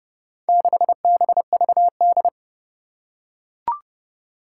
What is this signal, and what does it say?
Morse code, heard as a keyed tone, about 29 wpm, 700 Hz. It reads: 6BVD
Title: Four Challenging Letter-Number Combinations 30wpm